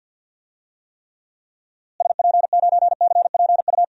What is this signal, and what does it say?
Morse code, heard as a keyed tone, about 51 wpm, 700 Hz. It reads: SP9CPU
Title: DXCC-100 Call Signs - 6 Characters - Repeated 3x - 50wpm